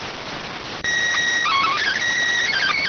SSTV (slow-scan television), PD 290.